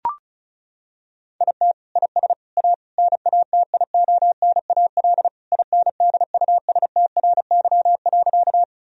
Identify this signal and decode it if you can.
Morse code, heard as a keyed tone, about 34 wpm, 700 Hz. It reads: IT IS A NATIONAL INDUSTRY.
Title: Sentences from Top 1000 Words - Review Part 4 - 35wpm